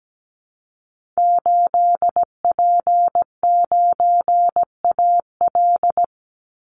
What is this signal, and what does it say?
Morse code, heard as a keyed tone, about 17 wpm, 700 Hz. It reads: 8P9AL
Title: Call Signs Repeated 3x - 17wpm